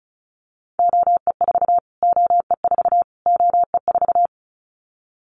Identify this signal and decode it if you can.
Morse code, heard as a keyed tone, about 34 wpm, 700 Hz. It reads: OE4 OE4 OE4
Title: Callsign Prefixes for Top 25 DXCC - Repeated 3x - 35wpm